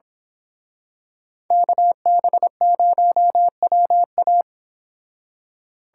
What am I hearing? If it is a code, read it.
Morse code, heard as a keyed tone, about 26 wpm, 700 Hz. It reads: KB0WA
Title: Speed Racing - USA 2x2 Call Signs 22wpm